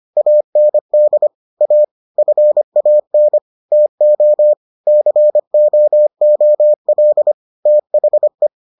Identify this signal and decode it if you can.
Morse code, heard as a keyed tone, about 25 wpm, 600 Hz. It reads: AND A FAN TO COOL THE